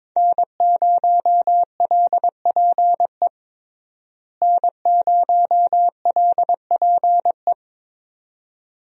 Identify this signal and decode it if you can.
Morse code, heard as a keyed tone, about 22 wpm, 700 Hz. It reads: N0LPE N0LPE
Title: USA Call Signs - 5 Characters - Repeated 2x - 22wpm